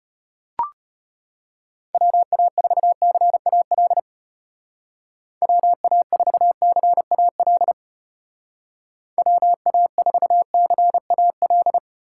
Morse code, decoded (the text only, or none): WA4CAL WA4CAL WA4CAL